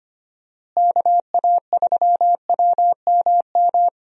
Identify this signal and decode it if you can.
Morse code, heard as a keyed tone, about 25 wpm, 700 Hz. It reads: KA3WMM
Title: USA Call Signs - 6 Characters - Repeated 2x - 25wpm